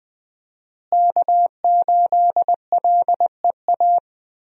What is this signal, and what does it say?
Morse code, heard as a keyed tone, about 21 wpm, 700 Hz. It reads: K8LEA
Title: USA Call Signs - 5 Characters - Repeated 3x - 20wpm